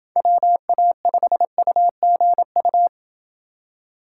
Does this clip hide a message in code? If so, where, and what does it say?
Yes: Morse code from 0.2 to 2.9 s, heard as a keyed tone, about 28 wpm, 700 Hz. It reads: WA5UGU